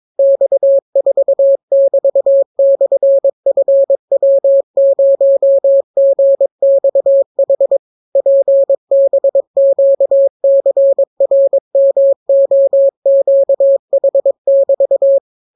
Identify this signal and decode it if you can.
Morse code, heard as a keyed tone, about 22 wpm, 550 Hz. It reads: X4<BT>/FW0GXH PBQCRMOQH<BT>